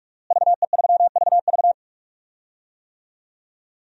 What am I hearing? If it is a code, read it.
Morse code, heard as a keyed tone, about 46 wpm, 700 Hz. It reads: VE3VV
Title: DXCC-100 Call Signs - 5 Characters - Repeated 3x - 45wpm